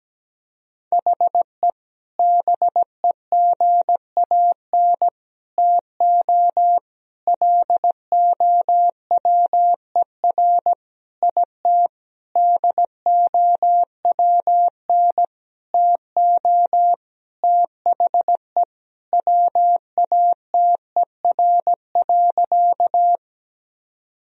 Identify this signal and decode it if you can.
Morse code, heard as a keyed tone, about 17 wpm, 700 Hz. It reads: HE BEGAN TO LOWER IT DOWN TO THE WATER.
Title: Sentences from Top 1000 Words - Review Part 4 - 17wpm